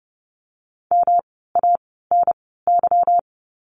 Morse code, heard as a keyed tone, about 30 wpm, spaced as for 17 wpm, 700 Hz. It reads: MANY
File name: course / 30:10wpm / 237 Lesson 38f - Letter q - Words 30wpm